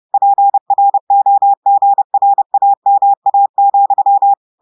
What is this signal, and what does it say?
Morse code, heard as a keyed tone, about 30 wpm, 800 Hz. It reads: PROGRAMA,
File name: audios / portugais / portugais texte en clair 2 vitesse 30 mots minutes